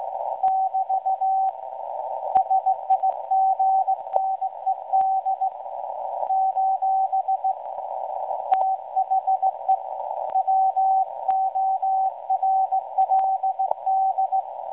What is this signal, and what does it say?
Morse code, heard as a keyed tone, about 17 wpm, 760 Hz. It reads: <BT> SIGNED : THE WORLD
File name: WebSDR records / SAQ at PI4THT msg 0800-0804 17.2kHz